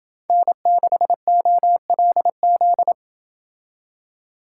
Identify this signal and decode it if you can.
Morse code, heard as a keyed tone, about 28 wpm, 700 Hz. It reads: N6OLZ